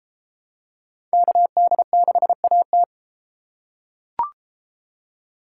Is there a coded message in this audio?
Yes: Morse code from 1.1 to 2.8 s, heard as a keyed tone, about 33 wpm, 700 Hz. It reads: KD6AT